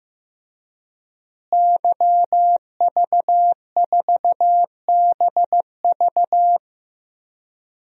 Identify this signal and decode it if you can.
Morse code, heard as a keyed tone, about 15 wpm, 700 Hz. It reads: YV4BV
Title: DXCC-100 Call Signs - 5 Characters - Repeated 3x - 15wpm